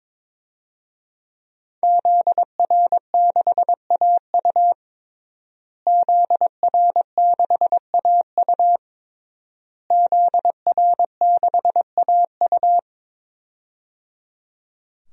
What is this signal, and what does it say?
Morse code, heard as a keyed tone, about 22 wpm, 700 Hz. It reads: ZR6AU ZR6AU ZR6AU
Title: DXCC-100 Call Signs - 5 Characters - Repeated 3x - 22wpm